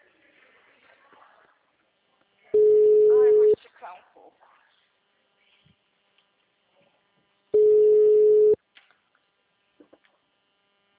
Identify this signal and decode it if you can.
Ringback tone (Europe, 425 Hz), with speech.